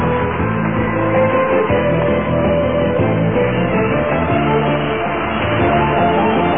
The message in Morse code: none